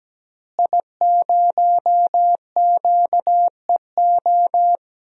Morse code, heard as a keyed tone, about 17 wpm, 700 Hz. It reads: I0QEO
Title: DXCC-100 Call Signs - 5 Characters - Repeated 3x - 17wpm